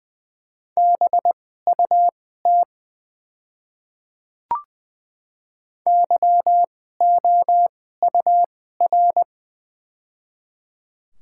Morse code, heard as a keyed tone, about 20 wpm, spaced as for 14 wpm, 700 Hz. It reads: BUT YOUR
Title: 2x Character Spacing - Top 200 Words - 20wpm